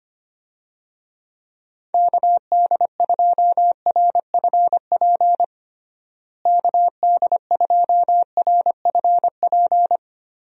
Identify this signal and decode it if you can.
Morse code, heard as a keyed tone, about 25 wpm, 700 Hz. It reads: KD2RFP KD2RFP
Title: USA Call Signs - 6 Characters - Repeated 3x - 25wpm